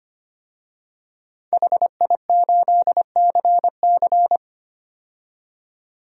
Morse code, heard as a keyed tone, about 25 wpm, 700 Hz. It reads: HI8CC
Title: DXCC-100 Call Signs - 5 Characters - 25wpm